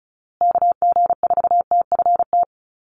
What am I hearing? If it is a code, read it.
Morse code, heard as a keyed tone, about 34 wpm, 700 Hz. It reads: KG4TFT
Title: USA Call Signs Repeated 2x 35wpm